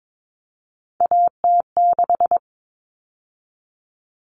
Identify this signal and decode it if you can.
Morse code, heard as a keyed tone, about 22 wpm, 700 Hz. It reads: AT6
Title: All Cut Numbers - 3-Digit 22wpm